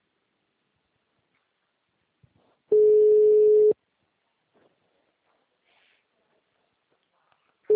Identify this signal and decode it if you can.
Ringback tone (Europe, 425 Hz).